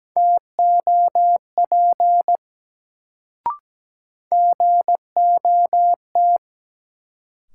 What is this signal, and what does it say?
Morse code, heard as a keyed tone, about 17 wpm, 700 Hz. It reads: TOP GOT
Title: Sets of 2 Words - New Word Drill - 17wpm